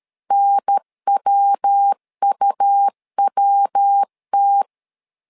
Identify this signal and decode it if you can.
Morse code, heard as a keyed tone, about 13 wpm, 800 Hz. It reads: NWUWT
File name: LEZIONI / 3 MWRO / 3 MW 12wpm 1633